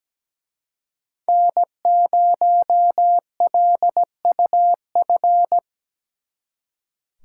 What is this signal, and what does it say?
Morse code, heard as a keyed tone, about 20 wpm, 700 Hz. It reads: N0LUF
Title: Speed Racing - USA 1x3 Call Signs 17wpm